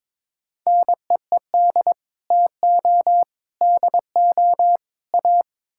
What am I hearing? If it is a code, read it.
Morse code, heard as a keyed tone, about 22 wpm, 700 Hz. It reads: NEED TO DO A
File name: practice / 22wpm / Sets of 7 Words - 22wpm